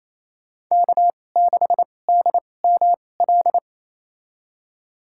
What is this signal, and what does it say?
Morse code, heard as a keyed tone, about 28 wpm, spaced as for 20 wpm, 700 Hz. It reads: K6DML
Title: USA Call Signs - 5 Characters - 2x Character Spacing - 28wpm